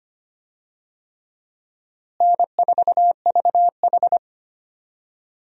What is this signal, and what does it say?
Morse code, heard as a keyed tone, about 25 wpm, 700 Hz. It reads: N4VH
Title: USA Call Signs - 4 Characters - Repeated 3x - 25wpm